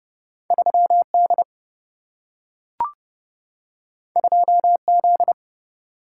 Morse code, heard as a keyed tone, about 30 wpm, 700 Hz. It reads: 3 D 2 Z
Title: Single-Letter Single-Number 30wpm